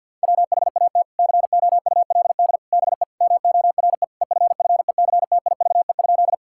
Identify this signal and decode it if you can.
Morse code, heard as a keyed tone, about 50 wpm, 700 Hz. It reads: WHAT COULD BE MORE EFFECTIVE?
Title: Sentences from Top 1000 Words - Review Part 6 - 50wpm